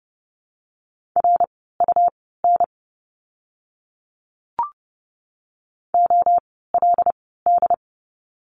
Morse code, heard as a keyed tone, about 30 wpm, spaced as for 17 wpm, 700 Hz. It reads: RUN OLD
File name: course / 30:10wpm / 115 Lesson 19f - Letter w - Words 30wpm